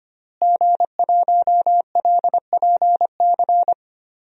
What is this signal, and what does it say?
Morse code, heard as a keyed tone, about 25 wpm, 700 Hz. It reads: G1LPC
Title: DXCC-100 Call Signs - 5 Characters - Repeated 2x - 25wpm